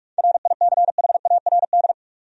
Morse code, heard as a keyed tone, about 44 wpm, 700 Hz. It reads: RICHARD
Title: USA Common Names - 45wpm